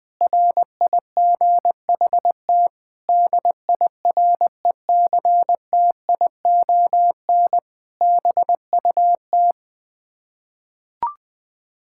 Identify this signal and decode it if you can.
Morse code, heard as a keyed tone, about 21 wpm, 700 Hz. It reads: RIGHT DIRECTION BUT
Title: Sets of 8 Words - 20wpm